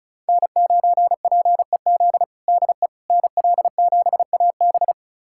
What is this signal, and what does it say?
Morse code, heard as a keyed tone, about 34 wpm, 700 Hz. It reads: N9PEZ DE NL7AB
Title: USA Callsign DE Callsign - 5 Characters - 35wpm